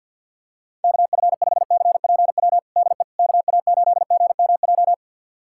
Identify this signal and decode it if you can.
Morse code, heard as a keyed tone, about 46 wpm, 700 Hz. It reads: KF5CPW DE KA8GMJ
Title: USA Callsign DE Callsign - 6 Characters - 50wpm